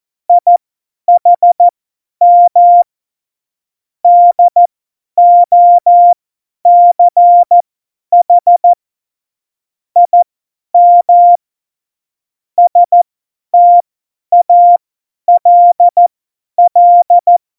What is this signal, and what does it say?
Morse code, heard as a keyed tone, about 14 wpm, spaced as for 10 wpm, 700 Hz. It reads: IHM DOCH IM STALL
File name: Morsekurs / Witze / Witz 10. Oktober 17, Tempo 50